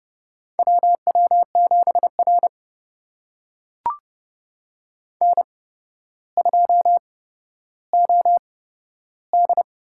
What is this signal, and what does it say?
Morse code, heard as a keyed tone, about 30 wpm, spaced as for 13 wpm, 700 Hz. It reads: WW7R N2OD
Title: USA Call Signs - 4 Characters - 8x Character Spacing - 30wpm